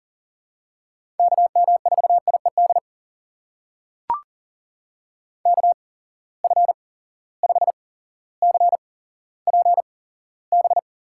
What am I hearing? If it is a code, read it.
Morse code, heard as a keyed tone, about 40 wpm, spaced as for 17 wpm, 700 Hz. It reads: KM4IED KF5CPB
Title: USA Call Signs - 6 Characters - 8x Character Spacing - 40wpm